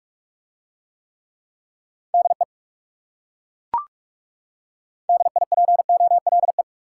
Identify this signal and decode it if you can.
Morse code, heard as a keyed tone, about 37 wpm, 700 Hz. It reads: DE DIPOLE
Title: Speed-Racing Top 90 QSO Elements 30wpm